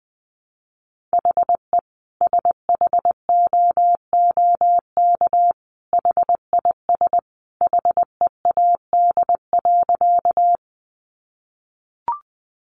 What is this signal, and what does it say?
Morse code, heard as a keyed tone, about 20 wpm, 700 Hz. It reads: HE SHOOK HIS HEAD.
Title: A Sound Of Thunder 20wpm